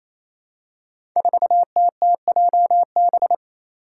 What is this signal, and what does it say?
Morse code, heard as a keyed tone, about 28 wpm, 700 Hz. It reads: 4TTJB
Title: Five Challenging Letter-Number Combinations 28wpm